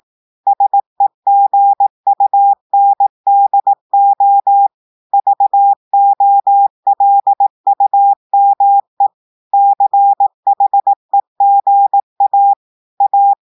Morse code, heard as a keyed tone, about 18 wpm, 800 Hz. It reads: SEGUNDO VOLUME CHEGA A
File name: audios / portugais / portugais texte en clair 1 vitesse 18 mots minutes